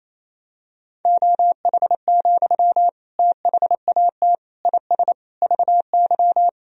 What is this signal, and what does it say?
Morse code, heard as a keyed tone, about 28 wpm, 700 Hz. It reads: OH, THAT IS VY
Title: Sentences from Top 500 Words and Abbreviations 28wpm